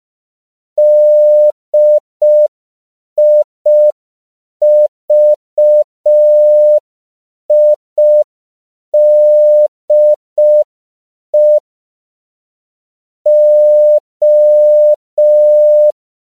Morse code, heard as a keyed tone, about 5 wpm, 600 Hz. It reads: DIVIDE O
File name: CW / 5PPM